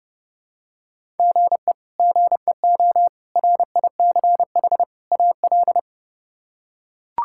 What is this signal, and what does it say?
Morse code, heard as a keyed tone, about 30 wpm, 700 Hz. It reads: GE GEO RICH AL
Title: K1USN SST QSO Practice - 30wpm